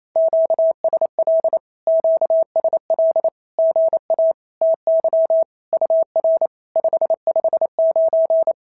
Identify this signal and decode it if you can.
Morse code, heard as a keyed tone, about 27 wpm, 690 Hz. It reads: QSL QSL GA TY UR 559
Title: SOTA QSO Practice 28wpm